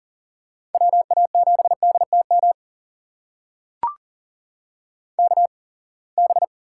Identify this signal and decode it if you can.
Morse code, heard as a keyed tone, about 39 wpm, spaced as for 27 wpm, 700 Hz. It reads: WA7DTM KB
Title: USA Call Signs - 6 Characters - 8x Character Spacing - 40wpm